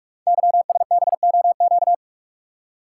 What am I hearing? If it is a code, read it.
Morse code, heard as a keyed tone, about 44 wpm, 700 Hz. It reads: YSBOQ
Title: Code Groups - 5 Letters - 45wpm